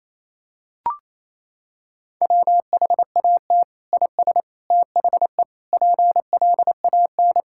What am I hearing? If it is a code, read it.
Morse code, heard as a keyed tone, about 27 wpm, 700 Hz. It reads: WHAT IS THE PLAN
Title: Sentences from Top 1000 Words - Review Part 5 - 28wpm